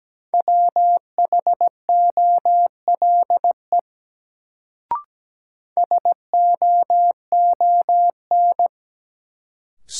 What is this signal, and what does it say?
Morse code, heard as a keyed tone, about 17 wpm, 700 Hz. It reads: WHOLE SOON
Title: Sentences from Top 400 Words - New Word Drill - 17wpm